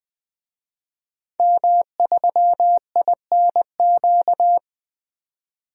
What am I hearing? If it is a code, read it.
Morse code, heard as a keyed tone, about 20 wpm, 700 Hz. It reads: M3INQ
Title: DXCC-100 Call Signs - 5 Characters - 20wpm